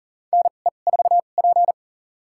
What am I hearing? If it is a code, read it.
Morse code, heard as a keyed tone, about 40 wpm, spaced as for 29 wpm, 700 Hz. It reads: NE4P
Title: USA Call Signs - 4 Characters - 2x Character Spacing - 40wpm